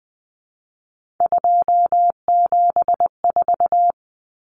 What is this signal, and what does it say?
Morse code, heard as a keyed tone, about 20 wpm, 700 Hz. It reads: 274
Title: Numbers - 3-Digit 20wpm